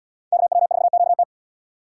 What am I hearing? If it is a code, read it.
Morse code, heard as a keyed tone, about 78 wpm, 700 Hz. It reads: KW4QT